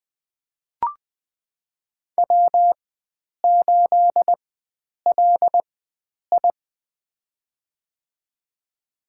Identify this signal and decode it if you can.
Morse code, heard as a keyed tone, about 20 wpm, spaced as for 9 wpm, 700 Hz. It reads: W8LI
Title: USA Call Signs - 4 Characters - 4x Character Spacing - 20wpm